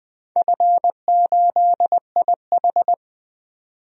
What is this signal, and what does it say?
Morse code, heard as a keyed tone, about 20 wpm, 700 Hz. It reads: F8IH
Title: DXCC-100 Call Signs - 4 Characters - 20wpm